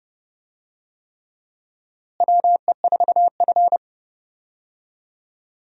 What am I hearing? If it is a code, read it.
Morse code, heard as a keyed tone, about 32 wpm, 700 Hz. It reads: WE4F
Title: USA Call Signs - 4 Characters - Repeated 3x - 30wpm